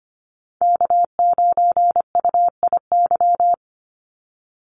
Morse code, heard as a keyed tone, about 25 wpm, 700 Hz. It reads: K9UIY
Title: Call Signs - 25wpm